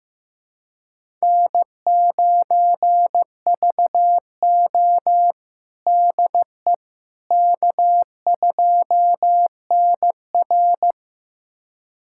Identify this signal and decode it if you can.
Morse code, heard as a keyed tone, about 15 wpm, 700 Hz. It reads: N9VO DE K2NR
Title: USA Callsign DE Callsign - 4 Characters - 15wpm